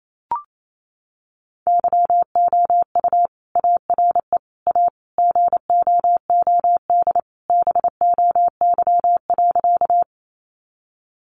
Morse code, heard as a keyed tone, about 28 wpm, 700 Hz. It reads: YOU ARE A GOOD BOY.
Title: Sentences from Top 200 Words 28wpm